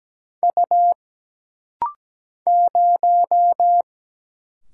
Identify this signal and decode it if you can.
Morse code, heard as a keyed tone, about 17 wpm, 700 Hz. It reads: U 0
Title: Single Letter-Number Alt - Rapid-Fire 17wpm